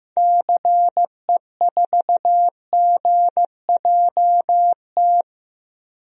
Morse code, heard as a keyed tone, about 15 wpm, 700 Hz. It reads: CE4GJT
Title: DXCC-100 Call Signs - 6 Characters - Repeated 3x - 15wpm